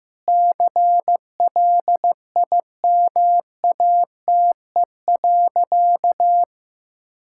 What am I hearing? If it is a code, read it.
Morse code, heard as a keyed tone, about 15 wpm, 700 Hz. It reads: CLIMATE.
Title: Sentences from Top 1600 Words 15wpm